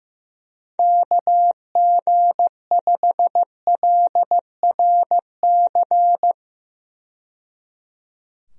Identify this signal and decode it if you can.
Morse code, heard as a keyed tone, about 15 wpm, 700 Hz. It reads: KG5LRC
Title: USA Call Signs - 6 Characters - 15wpm